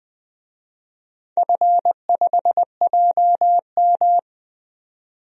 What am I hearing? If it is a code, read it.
Morse code, heard as a keyed tone, about 20 wpm, 700 Hz. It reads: F5JM
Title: DXCC-100 Call Signs - 4 Characters - 20wpm